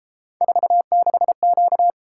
Morse code, heard as a keyed tone, about 33 wpm, 700 Hz. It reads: 46Q